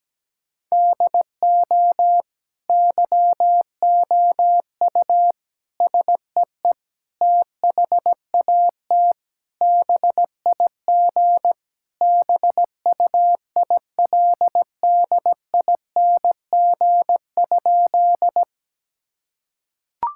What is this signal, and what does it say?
Morse code, heard as a keyed tone, about 17 wpm, 700 Hz. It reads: DO YOU SEE THAT BIG BUILDING?
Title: Sentences from Top 1000 Words - Review Part 5 - 17wpm